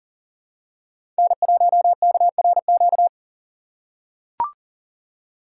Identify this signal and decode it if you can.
Morse code, heard as a keyed tone, about 40 wpm, 700 Hz. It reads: N1KRQ
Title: USA Call Signs - 2x Character Spacing - 40wpm